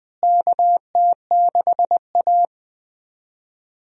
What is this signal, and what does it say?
Morse code, heard as a keyed tone, about 20 wpm, 700 Hz. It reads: KT6A
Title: USA Call Signs - 4 Characters - Repeated 2x - 20wpm